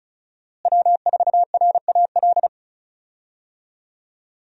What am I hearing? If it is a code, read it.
Morse code, heard as a keyed tone, about 34 wpm, 700 Hz. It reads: W4RAL
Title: USA Call Signs - 5 Characters - Repeated 2x - 35wpm